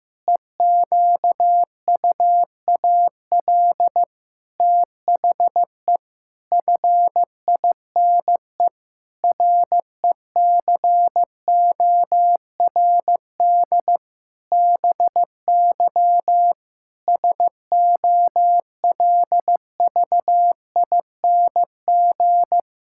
Morse code, heard as a keyed tone, about 15 wpm, 700 Hz. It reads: EQUAL THE FINE RECORD BY SOLVING
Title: Pangram Challenge v2 - 15wpm